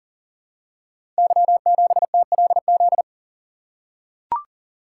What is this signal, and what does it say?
Morse code, heard as a keyed tone, about 42 wpm, 700 Hz. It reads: Y7TLZ
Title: Code Groups - 5 Letter:Numbers - 40wpm